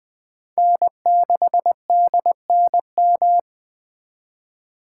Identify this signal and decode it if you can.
Morse code, heard as a keyed tone, about 20 wpm, 700 Hz. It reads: N6DNM
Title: 2023 SOTA Chaser Call Signs - 20wpm